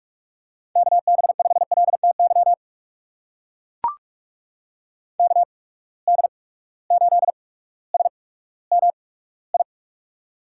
Morse code, heard as a keyed tone, about 43 wpm, spaced as for 19 wpm, 700 Hz. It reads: KB5LTY KD8SMI
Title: USA Call Signs - 6 Characters - 8x Character Spacing - 45wpm